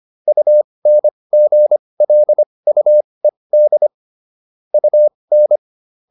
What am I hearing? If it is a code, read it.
Morse code, heard as a keyed tone, about 25 wpm, spaced as for 20 wpm, 600 Hz. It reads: UNGLUED UN